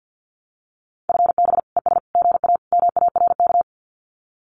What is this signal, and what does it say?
Morse code, heard as a keyed tone, about 50 wpm, 700 Hz. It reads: FB ES GA MARK
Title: Common QSO Phrases Encore 50wpm